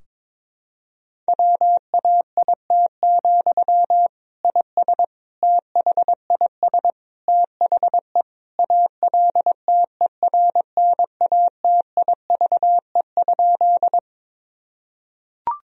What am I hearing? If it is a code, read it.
Morse code, heard as a keyed tone, about 22 wpm, 700 Hz. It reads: WAIT, IS THIS THE ALTERNATIVE?
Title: Sentences from Top 1000 Words - Review Part 4 - 22wpm